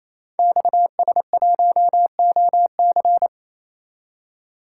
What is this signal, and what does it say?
Morse code, heard as a keyed tone, about 29 wpm, 700 Hz. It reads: XS1OC
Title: Code Groups - 5 Letter:Numbers - 28wpm